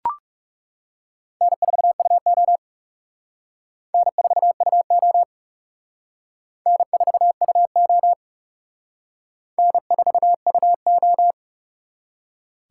Morse code, heard as a keyed tone, about 36 wpm, 700 Hz. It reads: N4UO N4UO N4UO N4UO
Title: Speed Racing - USA 1x2 Call Signs 30wpm